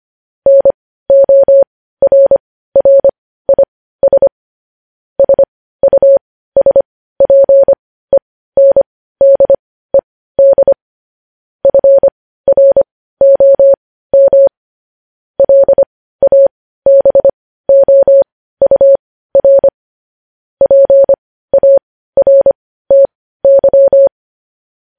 Morse code, heard as a keyed tone, about 25 wpm, spaced as for 15 wpm, 550 Hz. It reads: NORRIS SUSPENDED FROM LABOUR PARTY
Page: news-headlines-in-morse-code-at-15-wpm